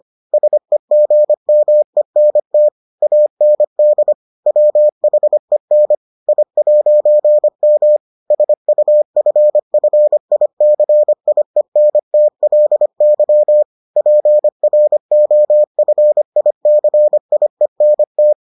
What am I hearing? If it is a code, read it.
Morse code, heard as a keyed tone, about 25 wpm, 600 Hz. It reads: SEGMENT AND WHEN I'M SUFFICIENTLY PROFICIENT